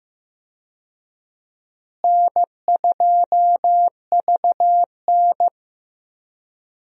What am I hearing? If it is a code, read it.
Morse code, heard as a keyed tone, about 15 wpm, 700 Hz. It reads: N2VN
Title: USA Call Signs - 4 Characters - Repeated 2x - 15wpm